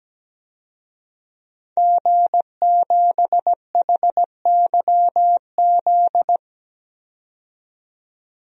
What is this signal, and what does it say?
Morse code, heard as a keyed tone, about 17 wpm, 700 Hz. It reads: G7HYZ
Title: DXCC-100 Call Signs - 5 Characters - 17wpm